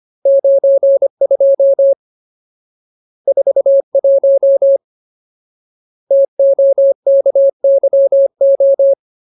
Morse code, heard as a keyed tone, about 25 wpm, 550 Hz. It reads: 92 41 TOKYO
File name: Ultimate Word List / 25WPM Ultimate